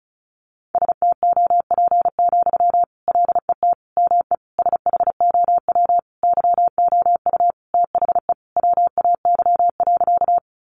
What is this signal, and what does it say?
Morse code, heard as a keyed tone, about 34 wpm, 700 Hz. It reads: STOP, LET ME SHOW YOU THE WAY.
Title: Sentences from Top 300 Words 35wpm